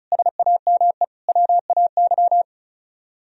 Morse code, heard as a keyed tone, about 43 wpm, 700 Hz. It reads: SAME WAY
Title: Speed-Racing - Sets of 4 Words 35wpm